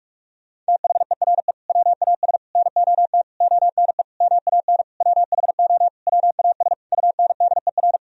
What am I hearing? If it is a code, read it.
Morse code, heard as a keyed tone, about 41 wpm, 700 Hz. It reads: THERE WAS NOT ONE MAN WHO WAS UNDER
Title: Sentences from Top 1000 Words - Review Part 2 - 45wpm